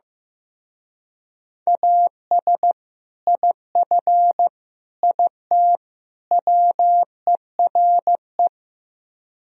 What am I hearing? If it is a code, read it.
Morse code, heard as a keyed tone, about 15 wpm, 700 Hz. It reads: AS IF IT WERE
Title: Speed-Racing - Sets of 4 Words 15wpm